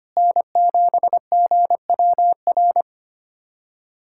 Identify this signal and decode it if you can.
Morse code, heard as a keyed tone, about 25 wpm, 700 Hz. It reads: N7GWR
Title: USA Callsign DE Callsign - 5 Characters - 25wpm